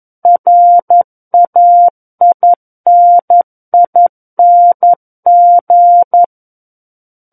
Morse code, heard as a keyed tone, about 11 wpm, 700 Hz. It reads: RAINING
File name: 11 WPM